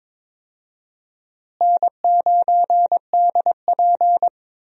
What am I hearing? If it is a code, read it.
Morse code, heard as a keyed tone, about 22 wpm, 700 Hz. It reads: N9DP
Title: USA Call Signs - 4 Characters - Repeated 3x - 22wpm